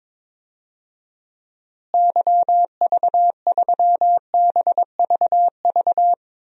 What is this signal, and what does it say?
Morse code, heard as a keyed tone, about 22 wpm, 700 Hz. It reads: YV3BVV
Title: DXCC-100 Call Signs - 6 Characters - Repeated 3x - 22wpm